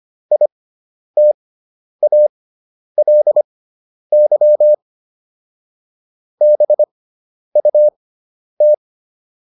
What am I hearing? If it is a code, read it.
Morse code, heard as a keyed tone, about 25 wpm, spaced as for 10 wpm, 600 Hz. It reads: ITALY BUT